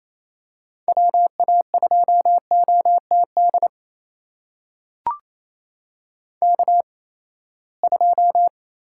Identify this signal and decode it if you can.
Morse code, heard as a keyed tone, about 28 wpm, spaced as for 19 wpm, 700 Hz. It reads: WA2OTD K2
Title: USA Call Signs - 8x Character Spacing - 28wpm